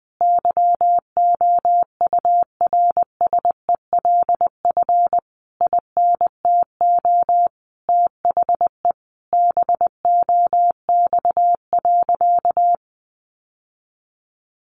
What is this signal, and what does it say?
Morse code, heard as a keyed tone, about 20 wpm, 700 Hz. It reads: YOURSELF INTO THE BOX.
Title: Sentences from Top 1000 Words 20wpm